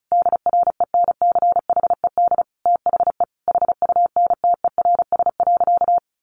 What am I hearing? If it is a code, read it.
Morse code, heard as a keyed tone, about 34 wpm, 700 Hz. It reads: DRENCHED THE HUNTERS.
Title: A Sound Of Thunder 35wpm